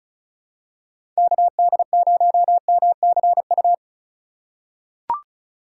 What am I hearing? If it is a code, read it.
Morse code, heard as a keyed tone, about 36 wpm, 700 Hz. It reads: KD0MCU
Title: USA Call Signs - 2x Character Spacing - 35wpm